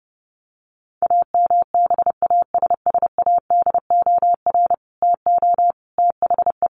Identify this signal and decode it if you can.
Morse code, heard as a keyed tone, about 30 wpm, 700 Hz. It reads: AMBASSADOR TO THE
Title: Sets of 5 Words Encore2 30wpm